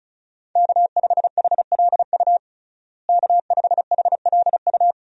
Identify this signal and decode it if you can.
Morse code, heard as a keyed tone, about 36 wpm, 700 Hz. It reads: K5HLU K5HLU
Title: USA Call Signs - 5 Characters - Repeated 2x - 35wpm